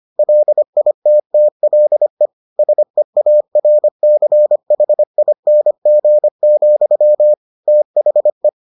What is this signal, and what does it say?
Morse code, heard as a keyed tone, about 25 wpm, 600 Hz. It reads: LITTLE SEARCHING, THE